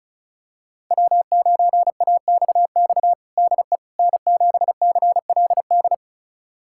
Morse code, heard as a keyed tone, about 36 wpm, 700 Hz. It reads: W9AXX DE N7CLD
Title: USA Callsign DE Callsign - 5 Characters - 35wpm